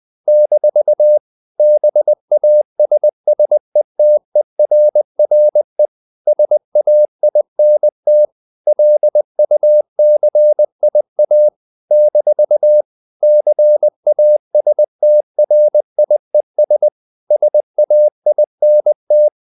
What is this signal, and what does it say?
Morse code, heard as a keyed tone, about 20 wpm, 600 Hz. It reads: - BASSETERRE SAINT LUCIA - CASTRIES SAINT
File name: CW / Practice / NationalCapitals20